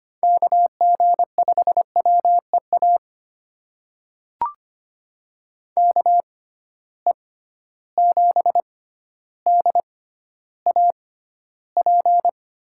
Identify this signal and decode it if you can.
Morse code, heard as a keyed tone, about 25 wpm, spaced as for 13 wpm, 700 Hz. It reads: KG5WEA KE7DAP
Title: USA Call Signs - 6 Characters - 6x Character Spacing - 25wpm